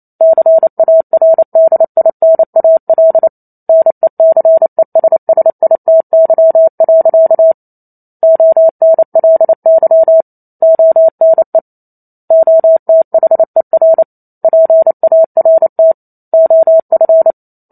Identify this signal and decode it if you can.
Morse code, heard as a keyed tone, about 28 wpm, 650 Hz. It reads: CARDINAL NECESSITY. ONLY ONE OTHER PART OF
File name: Chapter 2 - 28 WPM